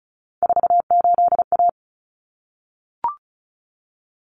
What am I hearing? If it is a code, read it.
Morse code, heard as a keyed tone, about 34 wpm, 700 Hz. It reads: 48A
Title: Common Cut Numbers - 3-Digit 35wpm